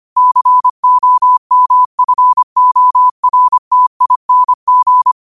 Morse code, heard as a keyed tone, about 25 wpm, 1000 Hz. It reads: COMFORTING
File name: qotdfiles / 25wpm